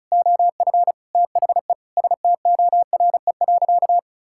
Morse code, heard as a keyed tone, about 34 wpm, 700 Hz. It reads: OF THE STORE.
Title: Sentences from Top 1000 Words - Review Part 5 - 35wpm